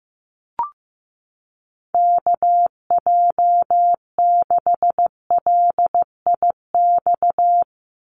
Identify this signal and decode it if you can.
Morse code, heard as a keyed tone, about 15 wpm, 700 Hz. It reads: KJ6LIX
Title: USA Call Signs 15wpm